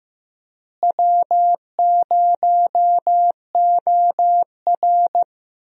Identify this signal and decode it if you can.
Morse code, heard as a keyed tone, about 15 wpm, 700 Hz. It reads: W0OR
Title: USA Call Signs - 4 Characters - Repeated 3x - 15wpm